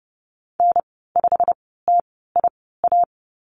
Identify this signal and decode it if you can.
Morse code, heard as a keyed tone, about 30 wpm, spaced as for 17 wpm, 700 Hz. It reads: N5TIA
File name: Lesson 15g - Number 5 - Call Signs 30wpm